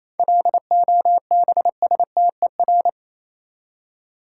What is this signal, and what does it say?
Morse code, heard as a keyed tone, about 27 wpm, 700 Hz. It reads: LOBSTER
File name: Easy Words - Speed-Racing with Hints - 22wpm